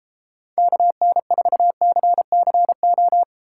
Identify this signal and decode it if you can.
Morse code, heard as a keyed tone, about 32 wpm, 700 Hz. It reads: KN4CCO